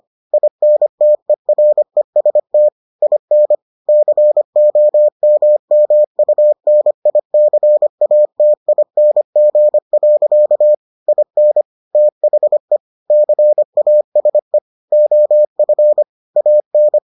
Morse code, heard as a keyed tone, about 25 wpm, 600 Hz. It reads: INTEREST IN COMMUNICATING. IN THE CASE OF AN